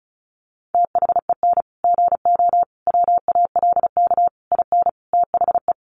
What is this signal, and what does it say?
Morse code, heard as a keyed tone, about 34 wpm, 700 Hz. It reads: THEN GO WALK IN THE
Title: Sentences from Top 1300 Words 35wpm